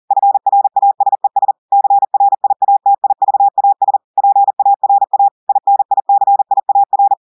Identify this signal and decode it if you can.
Morse code, heard as a keyed tone, about 40 wpm, 800 Hz. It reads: FRASES CRIATIVAS PARA INICIAR
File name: audios / portugais / portugais texte en clair 2 vitesse 40 mots minutes